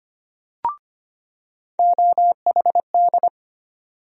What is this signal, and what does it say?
Morse code, heard as a keyed tone, about 25 wpm, 700 Hz. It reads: OHD
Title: USA Call Sign Postfixes - 3 Letters 25wpm